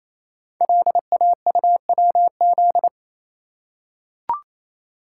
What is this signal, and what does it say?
Morse code, heard as a keyed tone, about 27 wpm, 700 Hz. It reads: LAUWZ
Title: Code Groups - 5 Letter:Numbers - 28wpm